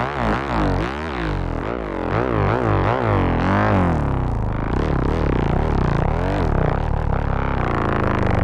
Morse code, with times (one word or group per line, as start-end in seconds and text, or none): none